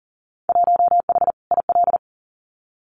Morse code, heard as a keyed tone, about 40 wpm, 700 Hz. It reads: 1H IL
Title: Winter Field Day Exchanges 40wpm